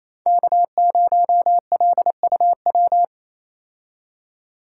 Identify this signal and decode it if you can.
Morse code, heard as a keyed tone, about 28 wpm, 700 Hz. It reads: K0LUW
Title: K1USN SST QSO Practice - 28wpm